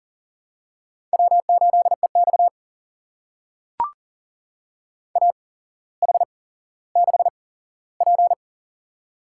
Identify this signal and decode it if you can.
Morse code, heard as a keyed tone, about 37 wpm, spaced as for 17 wpm, 700 Hz. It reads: W8EX AH6P
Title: USA Call Signs - 4 Characters - 8x Character Spacing - 40wpm